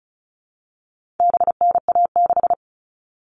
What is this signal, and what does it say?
Morse code, heard as a keyed tone, about 34 wpm, 700 Hz. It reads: BNA6
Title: All Cut Numbers - 4-Digit 35wpm